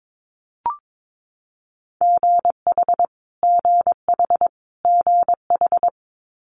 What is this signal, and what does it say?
Morse code, heard as a keyed tone, about 22 wpm, 700 Hz. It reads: GH GH GH
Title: Multi-letter Phonemes Repeated 3x 22wpm